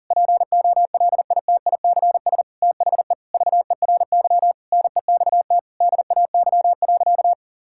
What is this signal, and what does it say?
Morse code, heard as a keyed tone, about 40 wpm, 700 Hz. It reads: POLITICS THE VERY NEXT DAY.
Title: Sentences from Top 1000 Words - Review Part 6 - 40wpm